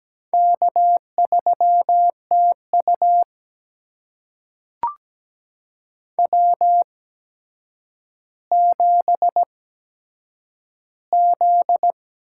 Morse code, heard as a keyed tone, about 17 wpm, spaced as for 8 wpm, 700 Hz. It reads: K3TU W7Z
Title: USA Call Signs - 4 Characters - 8x Character Spacing - 17wpm